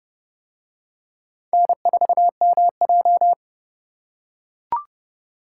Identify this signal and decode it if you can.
Morse code, heard as a keyed tone, about 28 wpm, 700 Hz. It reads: N4MJ